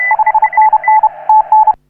Morse code, heard as a keyed tone, about 30 wpm, 870 Hz. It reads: 5NN TN
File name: CW / HA3OU RHOBCW REC